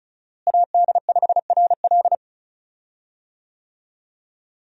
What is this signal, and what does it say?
Morse code, heard as a keyed tone, about 36 wpm, 700 Hz. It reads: AD5RL
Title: USA Call Signs - 5 Characters - Repeated 3x - 35wpm